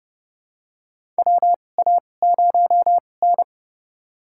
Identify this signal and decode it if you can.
Morse code, heard as a keyed tone, about 30 wpm, spaced as for 22 wpm, 700 Hz. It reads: WA0N
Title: USA Call Signs - 4 Characters - 2x Character Spacing - 30wpm